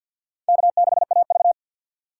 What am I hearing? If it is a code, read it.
Morse code, heard as a keyed tone, about 48 wpm, 700 Hz. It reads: K6AV
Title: USA Call Signs - 4 Characters - 2x Character Spacing - 50wpm